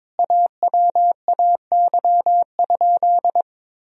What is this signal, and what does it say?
Morse code, heard as a keyed tone, about 21 wpm, 700 Hz. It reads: AWAY?
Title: Sentences from Top 1000 Words - Review Part 1 - 22wpm